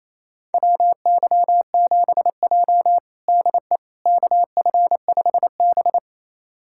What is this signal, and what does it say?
Morse code, heard as a keyed tone, about 28 wpm, 700 Hz. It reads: WY7J DE KF5B
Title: USA Callsign DE Callsign - 4 Characters - 28wpm